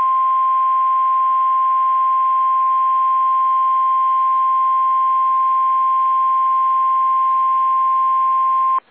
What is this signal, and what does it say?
Emergency Alert System (SAME) tones (the 1050 Hz attention tone). It is heard from 0.7 to 7.8 s.
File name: eas / KSDZKDJL / KSDZKDJL